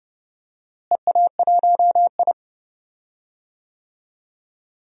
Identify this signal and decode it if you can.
Morse code, heard as a keyed tone, about 30 wpm, 700 Hz. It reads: EA1I
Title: DXCC-100 Call Signs - 4 Characters - Repeated 3x - 30wpm